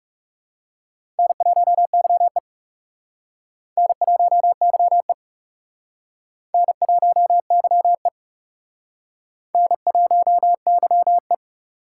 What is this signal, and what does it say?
Morse code, heard as a keyed tone, about 36 wpm, 700 Hz. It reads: N1YE N1YE N1YE N1YE
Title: Speed Racing - USA 1x2 Call Signs 30wpm